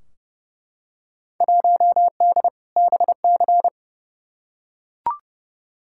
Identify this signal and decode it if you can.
Morse code, heard as a keyed tone, about 30 wpm, 700 Hz. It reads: 1D BC
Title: Field Day Exchanges - 30wpm